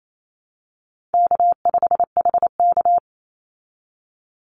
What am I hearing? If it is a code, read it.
Morse code, heard as a keyed tone, about 28 wpm, 700 Hz. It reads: K5HK
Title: Call Signs Repeated 2x - 28wpm